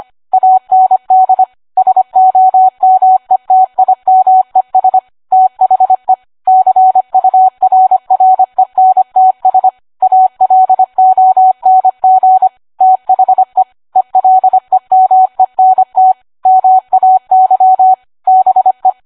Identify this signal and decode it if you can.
Morse code, heard as a keyed tone, about 25 wpm, 750 Hz. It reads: AND SOMETIMES THE CURRENTS ALONG THE ELEMENT MAY BE